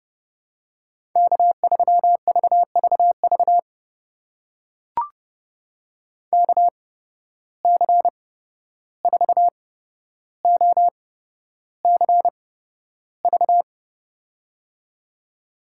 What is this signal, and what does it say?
Morse code, heard as a keyed tone, about 30 wpm, spaced as for 12 wpm, 700 Hz. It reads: K3VVV KC4OCV
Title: USA Call Signs - 8x Character Spacing - 30wpm